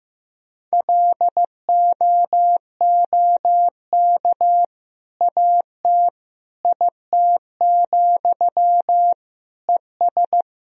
Morse code, heard as a keyed tone, about 15 wpm, 700 Hz. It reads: LOOK AT IT, ES
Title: Sentences from Top 200 Words and Abbreviations 15wpm